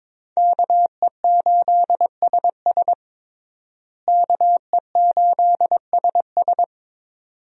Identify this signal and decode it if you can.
Morse code, heard as a keyed tone, about 22 wpm, 700 Hz. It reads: KE8SS KE8SS
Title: USA Call Signs - 5 Characters - Repeated 2x - 22wpm